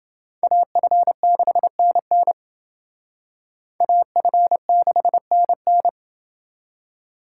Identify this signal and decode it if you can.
Morse code, heard as a keyed tone, about 28 wpm, 700 Hz. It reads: AF6NN AF6NN